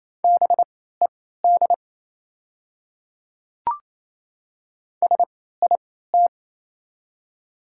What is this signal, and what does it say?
Morse code, heard as a keyed tone, about 28 wpm, spaced as for 16 wpm, 690 Hz. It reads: BED SIT
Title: 3x Character Spacing - Top 500 Words - 28wpm